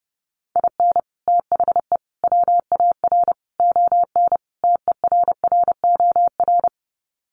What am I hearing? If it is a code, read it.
Morse code, heard as a keyed tone, about 30 wpm, 700 Hz. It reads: IN THE WAR ON TERROR
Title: Sets of 5 Words Encore2 30wpm